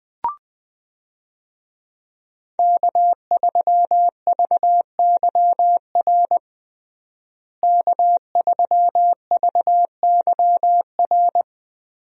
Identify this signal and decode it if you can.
Morse code, heard as a keyed tone, about 20 wpm, 700 Hz. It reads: K3VYR K3VYR
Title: USA Call Signs - 5 Characters - Repeated 3x - 20wpm